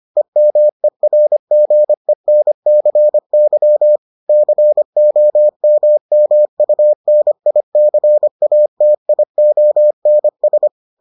Morse code, heard as a keyed tone, about 25 wpm, 600 Hz. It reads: EMERGENCY COMMUNICATIONS